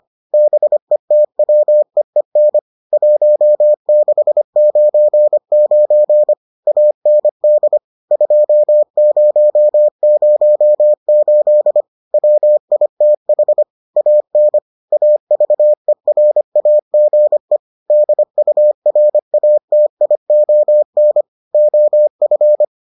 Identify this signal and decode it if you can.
Morse code, heard as a keyed tone, about 25 wpm, 600 Hz. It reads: BETWEEN 1699 AND 2008 WITH AN AVERAGE DURATION OF